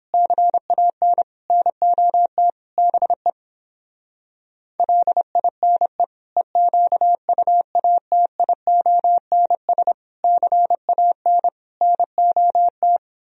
Morse code, heard as a keyed tone, about 29 wpm, 700 Hz. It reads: CAN NOT BE LINE EQUATIONS CAN NOT